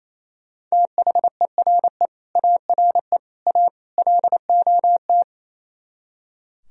Morse code, heard as a keyed tone, about 33 wpm, 700 Hz. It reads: THERE ARE A LOT
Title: Speed-Racing - Sets of 4 Words 28wpm